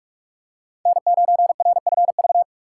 Morse code, heard as a keyed tone, about 44 wpm, 700 Hz. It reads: N9RFV